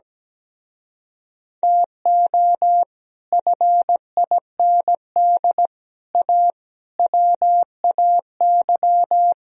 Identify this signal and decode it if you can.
Morse code, heard as a keyed tone, about 17 wpm, 700 Hz. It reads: TO FIND A WAY
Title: Speed-Racing - Sets of 5 Words 17wpm